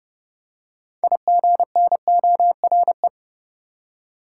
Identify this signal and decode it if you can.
Morse code, heard as a keyed tone, about 29 wpm, 700 Hz. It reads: IGNORE
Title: Six Letter Words - 30wpm